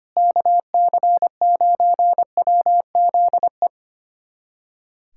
Morse code, heard as a keyed tone, about 25 wpm, 700 Hz. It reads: KC9WZE
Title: USA Callsign DE Callsign - 6 Characters - 25wpm